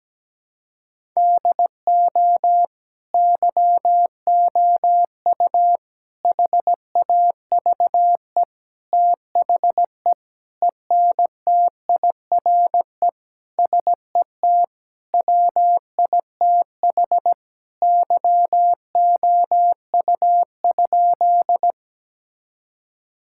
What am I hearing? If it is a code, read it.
Morse code, heard as a keyed tone, about 17 wpm, 700 Hz. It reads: DO YOU HAVE THE ENTIRE SET WITH YOU?
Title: Sentences from Top 1000 Words - Review Part 5 - 17wpm